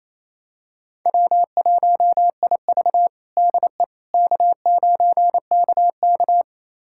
Morse code, heard as a keyed tone, about 28 wpm, 700 Hz. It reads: W1IV DE K9KK
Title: USA Callsign DE Callsign - 4 Characters - 28wpm